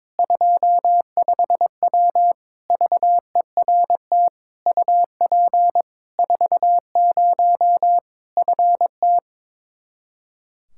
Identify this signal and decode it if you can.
Morse code, heard as a keyed tone, about 22 wpm, 700 Hz. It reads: 25W VERT UP 40 FT
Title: K1USN SST QSO Practice 22wpm